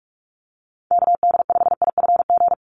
Morse code, heard as a keyed tone, about 46 wpm, 690 Hz. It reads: KD5IFG
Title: USA Call Signs 45wpm